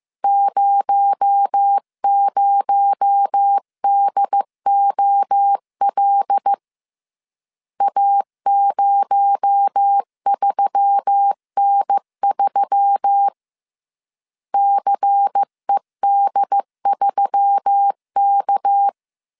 Morse code, heard as a keyed tone, about 15 wpm, 800 Hz. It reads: 00DOL A03N3 CED3K
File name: CW / DET / 1 6 1500C S C 14wpm 2951